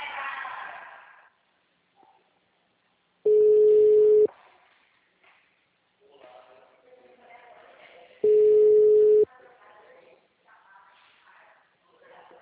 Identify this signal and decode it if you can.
Ringback tone (Europe, 425 Hz), with speech.